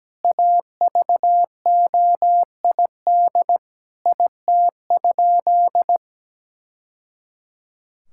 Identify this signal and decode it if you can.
Morse code, heard as a keyed tone, about 17 wpm, 700 Hz. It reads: AVOID IT?
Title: Sentences from Top 1000 Words - Review Part 6 - 17wpm